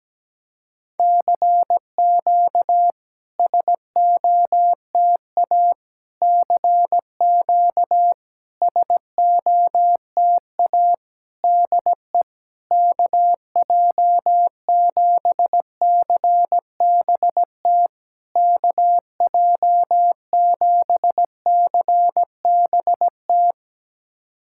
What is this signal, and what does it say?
Morse code, heard as a keyed tone, about 17 wpm, 700 Hz. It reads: CQ SOTA CQ SOTA DE KJ7CBT KJ7CBT
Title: SOTA QSO Practice 17wpm